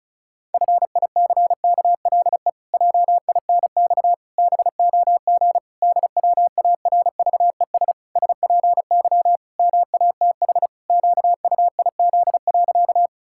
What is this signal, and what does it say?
Morse code, heard as a keyed tone, about 36 wpm, 700 Hz. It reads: FICKLE JINX BOG DWARVES SPY MATH QUIZ.
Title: Pangram Challenge v2 - 35wpm